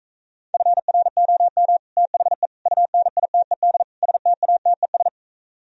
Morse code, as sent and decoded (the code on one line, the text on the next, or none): ..-. .-. --- -- / - .... / ..- -. .. - . -.. / ... - .- - . ...
FROM THE UNITED STATES